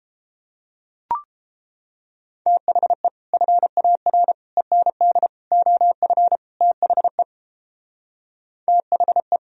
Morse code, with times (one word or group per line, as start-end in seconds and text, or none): 2.5-3.1 THE
3.3-4.3 FAR
4.6-5.3 END
5.5-6.4 OF
6.6-7.2 THE
8.7-9.4 THE